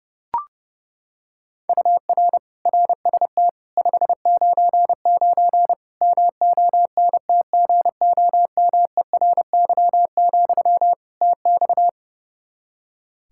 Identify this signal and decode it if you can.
Morse code, heard as a keyed tone, about 32 wpm, 700 Hz. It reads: UR RST 599 MONTGOMERY, TX
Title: K1USN SST QSO Practice 30wpm